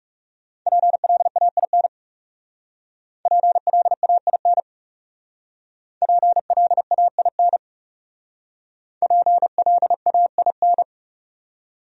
Morse code, heard as a keyed tone, about 38 wpm, 700 Hz. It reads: PLAIN PLAIN PLAIN PLAIN